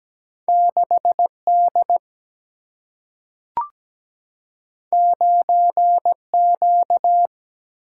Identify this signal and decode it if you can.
Morse code, heard as a keyed tone, about 17 wpm, 700 Hz. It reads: 6D 9Q
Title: Single-Letter Single-Number 17wpm